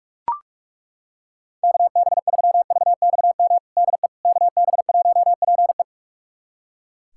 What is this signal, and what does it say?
Morse code, heard as a keyed tone, about 42 wpm, 700 Hz. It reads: KB3VXM DE KB1PE
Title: USA Callsign DE Callsign 45wpm